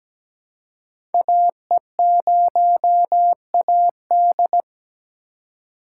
Morse code, heard as a keyed tone, about 17 wpm, 700 Hz. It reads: AE0AD
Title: Code Groups - 5 Letter:Numbers - 17wpm